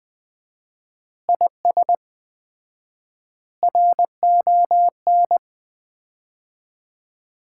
Morse code, heard as a keyed tone, about 20 wpm, 700 Hz. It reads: IS RON
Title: Common QSO Phrases - 4x Word-Spacing - 20wpm